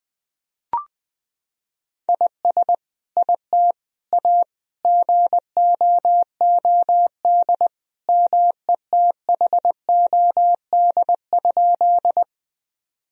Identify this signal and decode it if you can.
Morse code, heard as a keyed tone, about 20 wpm, 700 Hz. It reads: IS IT A GOOD METHOD?
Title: Sentences from Top 1000 Words - Review Part 3 - 20wpm